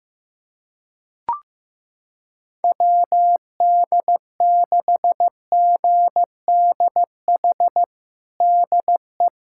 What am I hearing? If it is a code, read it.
Morse code, heard as a keyed tone, about 15 wpm, 700 Hz. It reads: WD6GDH DE
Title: USA Callsign DE Callsign - 6 Characters - 15wpm